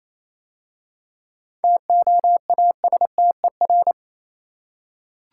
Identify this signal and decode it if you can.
Morse code, heard as a keyed tone, about 27 wpm, 700 Hz. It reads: TOASTER
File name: Easy Words - Speed-Racing with Hints - 22wpm